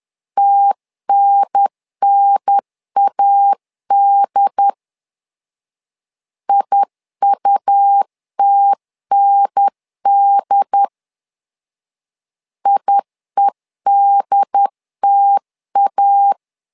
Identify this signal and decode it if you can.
Morse code, heard as a keyed tone, about 10 wpm, 800 Hz. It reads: TNNAD IUTND IEDTA
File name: LEZIONI / 2 UHDN / 2 UHDN 10wpm 1904